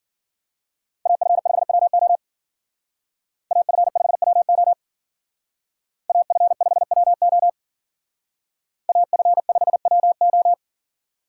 Morse code, heard as a keyed tone, about 52 wpm, 700 Hz. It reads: AF5WO AF5WO AF5WO AF5WO